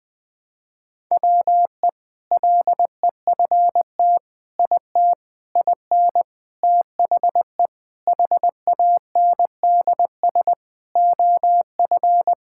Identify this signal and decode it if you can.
Morse code, heard as a keyed tone, about 20 wpm, 700 Hz. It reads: WE LEFT IT IN THE HANDS OF
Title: Sentences with 4-Word Phrases - 20wpm